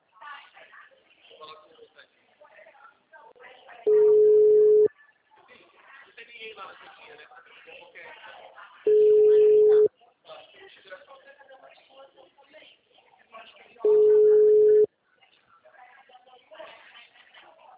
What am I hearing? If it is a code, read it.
Ringback tone (Europe, 425 Hz), with speech.
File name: out- -2009- - - .1500